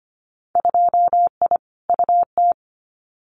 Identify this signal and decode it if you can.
Morse code, heard as a keyed tone, about 25 wpm, 690 Hz. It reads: 2I UT
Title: Winter Field Day Exchanges 25wpm